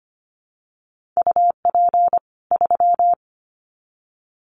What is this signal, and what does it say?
Morse code, heard as a keyed tone, about 25 wpm, 700 Hz. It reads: UP 3
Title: Common QSO Phrases 25wpm